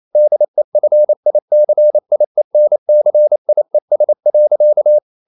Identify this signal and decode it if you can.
Morse code, heard as a keyed tone, about 28 wpm, 600 Hz. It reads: DEFICIENCIES.
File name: CW / Practice / EnglishQuotations0228